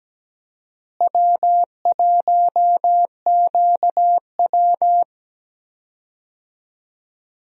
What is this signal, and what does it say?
Morse code, heard as a keyed tone, about 17 wpm, 700 Hz. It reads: W1QW
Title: USA Call Signs - 4 Characters - Repeated 2x - 17wpm